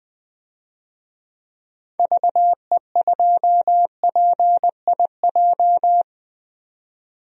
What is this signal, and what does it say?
Morse code, heard as a keyed tone, about 20 wpm, 700 Hz. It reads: VE2PIJ
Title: DXCC-100 Call Signs - 6 Characters - 20wpm